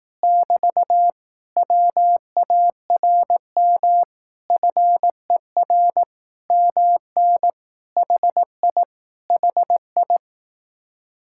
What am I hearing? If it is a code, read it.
Morse code, heard as a keyed tone, about 18 wpm, 700 Hz. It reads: <BT> WARM FER MN HI HI
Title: Speed-Racing Common QSO Phrases Encore - 15wpm